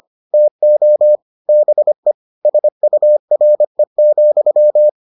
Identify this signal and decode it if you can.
Morse code, heard as a keyed tone, about 25 wpm, 600 Hz. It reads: TO BE SURE,